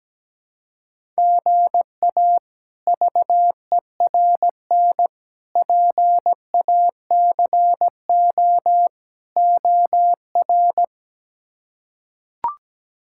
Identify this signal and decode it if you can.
Morse code, heard as a keyed tone, about 17 wpm, 700 Hz. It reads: GA VERN PACO OR
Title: K1USN SST QSO Practice - 17wpm